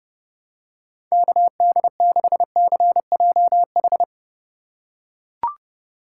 Morse code, heard as a keyed tone, about 30 wpm, 700 Hz. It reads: KD6CJH
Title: USA Call Signs - 2x Character Spacing - 30wpm